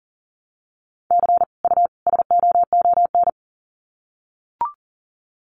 Morse code, heard as a keyed tone, about 40 wpm, 700 Hz. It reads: C U SOON
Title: Common QSO Phrases 40wpm